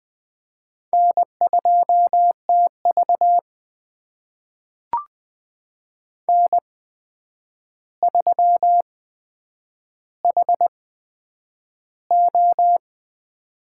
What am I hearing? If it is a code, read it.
Morse code, heard as a keyed tone, about 20 wpm, spaced as for 9 wpm, 700 Hz. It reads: N2TV N3HO
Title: USA Call Signs - 4 Characters - 8x Character Spacing - 20wpm